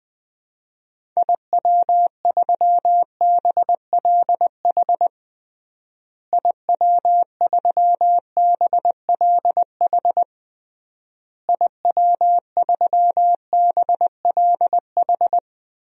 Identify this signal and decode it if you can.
Morse code, heard as a keyed tone, about 20 wpm, 700 Hz. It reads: IW3BLH IW3BLH IW3BLH
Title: DXCC-100 Call Signs - 6 Characters - Repeated 3x - 20wpm